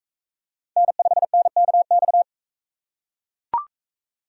Morse code, heard as a keyed tone, about 41 wpm, 700 Hz. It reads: N5NKX